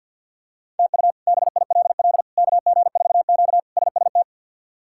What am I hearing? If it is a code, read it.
Morse code, heard as a keyed tone, about 49 wpm, 700 Hz. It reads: TU BILL KG4Q SST
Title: K1USN SST QSO Practice - 50wpm